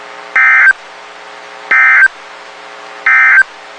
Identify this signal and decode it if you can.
Emergency Alert System (SAME) tones, heard as harsh buzzing bursts. It is heard from 0.2 to 3.5 s.
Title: WXK91 NOAA Weather Radio